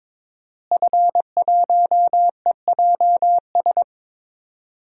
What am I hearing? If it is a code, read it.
Morse code, heard as a keyed tone, about 22 wpm, 700 Hz. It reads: F1EJS
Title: DXCC-100 Call Signs - 5 Characters - Repeated 3x - 22wpm